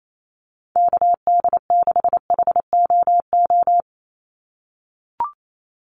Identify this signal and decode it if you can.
Morse code, heard as a keyed tone, about 28 wpm, 700 Hz. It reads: KD6HOO
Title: USA Call Signs 28wpm